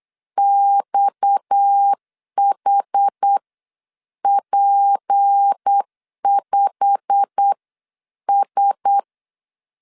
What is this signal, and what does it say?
Morse code, heard as a keyed tone, about 8 wpm, spaced as for 7 wpm, 800 Hz. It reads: XHP5S
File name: LEZIONI / 10 SIGNS / 10A SLASH 8wpm 3452